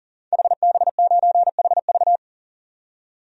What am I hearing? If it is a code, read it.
Morse code, heard as a keyed tone, about 37 wpm, 700 Hz. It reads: HB9HV
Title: DXCC-100 Call Signs - 5 Characters - 40wpm